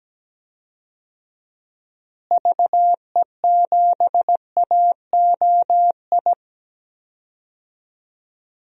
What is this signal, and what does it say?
Morse code, heard as a keyed tone, about 17 wpm, 700 Hz. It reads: VE7AOI
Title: DXCC-100 Call Signs - 6 Characters - Repeated 3x - 17wpm